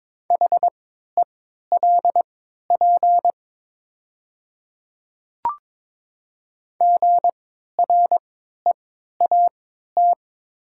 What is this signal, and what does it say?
Morse code, heard as a keyed tone, about 22 wpm, spaced as for 12 wpm, 700 Hz. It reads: HELP GREAT
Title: 3x Character Spacing - Top 200 Words - 22wpm